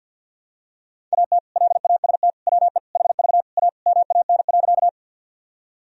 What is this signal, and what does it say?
Morse code, heard as a keyed tone, about 50 wpm, 700 Hz. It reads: AT LAST WE HV A MAN.
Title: Sentences from Top 300 Words and Abbreviations 50wpm